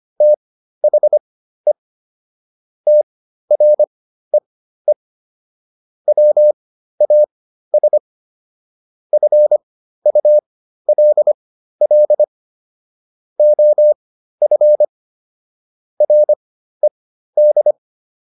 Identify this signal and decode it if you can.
Morse code, heard as a keyed tone, about 25 wpm, spaced as for 13 wpm, 600 Hz. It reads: THE TREE WAS FULL OF RED